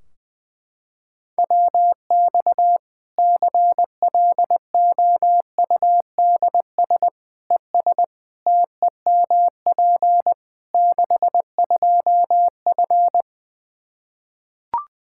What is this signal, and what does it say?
Morse code, heard as a keyed tone, about 20 wpm, 700 Hz. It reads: WX CLOUDS ES TEMP 62F
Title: K1USN SST QSO Practice 20wpm